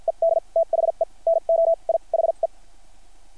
Morse code, heard as a keyed tone, about 50 wpm, spaced as for 40 wpm, 660 Hz. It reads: ED THE NOI HE
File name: archived / CRSnet